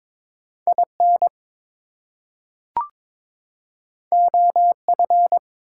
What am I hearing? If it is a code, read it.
Morse code, heard as a keyed tone, about 22 wpm, 700 Hz. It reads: IN OF
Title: Two Letter Words - 22wpm